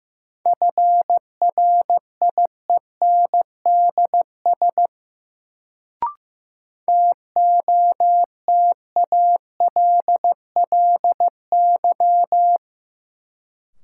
Morse code, heard as a keyed tone, about 15 wpm, 700 Hz. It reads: FRIENDS TOTALLY
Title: Sentences from Top 1600 Words - New Word Drill Alt - 15wpm